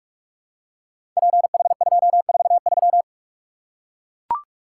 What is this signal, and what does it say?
Morse code, heard as a keyed tone, about 44 wpm, 700 Hz. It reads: PH243
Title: Code Groups - 5 Letter:Numbers - 45wpm